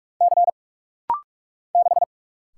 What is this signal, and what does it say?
Morse code, heard as a keyed tone, about 44 wpm, 700 Hz. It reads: C 6
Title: Single Letter-Number Alt - Rapid-Fire 45wpm